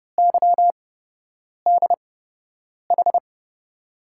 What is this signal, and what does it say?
Morse code, heard as a keyed tone, about 30 wpm, spaced as for 8 wpm, 700 Hz. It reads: YDH
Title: USA Call Signs - 6 Characters - 8x Character Spacing - 30wpm